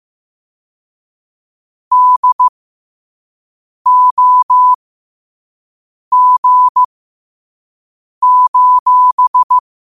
Morse code, heard as a keyed tone, about 15 wpm, spaced as for 5 wpm, 1000 Hz. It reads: DOG:
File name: qotdfiles / 5wpm